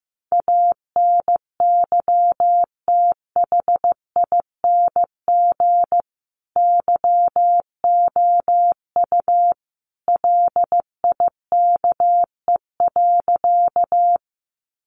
Morse code, heard as a keyed tone, about 15 wpm, 700 Hz. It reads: ANYTHING YOU LIKE.
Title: Sentences from Top 1000 Words 15wpm